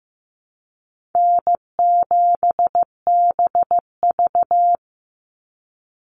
Morse code, heard as a keyed tone, about 15 wpm, 700 Hz. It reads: N7BV
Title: Easy Call Signs Repeated 2x - 15wpm